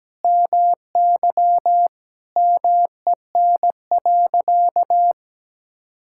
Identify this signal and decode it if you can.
Morse code, heard as a keyed tone, about 17 wpm, 700 Hz. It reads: MY MEN.
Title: Sentences from Top 1000 Words - Review Part 6 - 17wpm